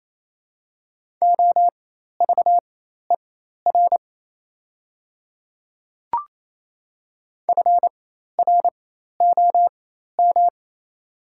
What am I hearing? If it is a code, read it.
Morse code, heard as a keyed tone, about 28 wpm, spaced as for 13 wpm, 700 Hz. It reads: OVER FROM
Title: 4x Character Spacing - Top 100 Words - 28wpm